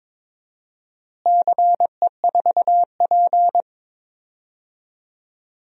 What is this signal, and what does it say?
Morse code, heard as a keyed tone, about 22 wpm, 700 Hz. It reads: CE4P
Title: DXCC-100 Call Signs - 4 Characters - Repeated 2x - 22wpm